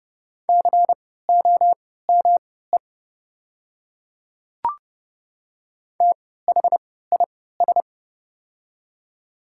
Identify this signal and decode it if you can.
Morse code, heard as a keyed tone, about 30 wpm, spaced as for 17 wpm, 700 Hz. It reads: COME THIS
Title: 3x Character Spacing - Top 100 Words - 30wpm